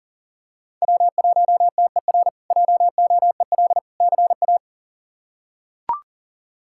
Morse code, heard as a keyed tone, about 40 wpm, 700 Hz. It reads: W1TER JOEL CA
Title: K1USN SST QSO Practice - 40wpm